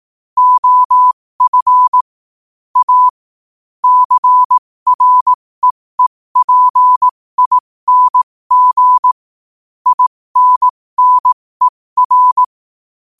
Morse code, heard as a keyed tone, about 18 wpm, spaced as for 15 wpm, 1000 Hz. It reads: OF A CREEPING INNER
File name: qotdfiles / 15wpm